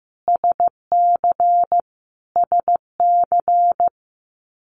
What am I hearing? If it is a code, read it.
Morse code, heard as a keyed tone, about 15 wpm, 700 Hz. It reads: SC SC
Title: US States Repeated 3x 15wpm